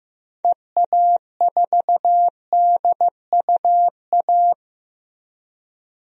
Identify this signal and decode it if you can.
Morse code, heard as a keyed tone, about 15 wpm, 700 Hz. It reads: EA4DUA
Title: DXCC-100 Call Signs - 6 Characters - Repeated 3x - 15wpm